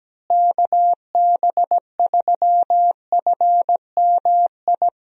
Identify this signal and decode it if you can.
Morse code, heard as a keyed tone, about 17 wpm, 700 Hz. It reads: KB3FMI
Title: USA Call Signs - 6 Characters - Repeated 2x - 17wpm